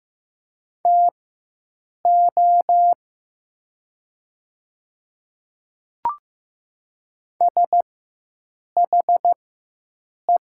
Morse code, heard as a keyed tone, about 15 wpm, spaced as for 7 wpm, 700 Hz. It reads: TO SHE
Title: 4x Character Spacing - Top 100 Words - 15wpm